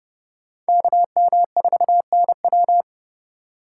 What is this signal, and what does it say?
Morse code, heard as a keyed tone, about 29 wpm, 700 Hz. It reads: KM4NW
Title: USA Call Signs - 5 Characters - 3x Character Spacing - 30wpm